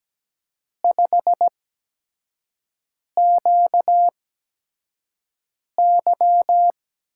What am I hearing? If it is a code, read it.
Morse code, heard as a keyed tone, about 17 wpm, spaced as for 5 wpm, 700 Hz. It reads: 5QY
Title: USA Call Signs - 4 Characters - 8x Character Spacing - 17wpm